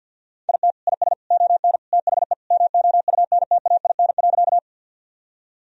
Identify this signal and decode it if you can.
Morse code, heard as a keyed tone, about 51 wpm, 700 Hz. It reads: IT IS ON THE MOUNTAIN.
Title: Sentences from Top 1000 Words - Review Part 1 - 50wpm